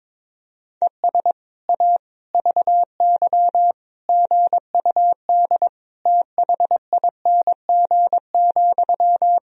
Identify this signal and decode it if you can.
Morse code, heard as a keyed tone, about 21 wpm, 700 Hz. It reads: ES A VY GUD THING,
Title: Sentences from Top 200 Words and Abbreviations 22wpm